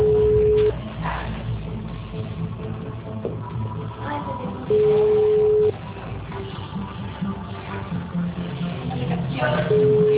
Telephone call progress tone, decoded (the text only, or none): ringback tone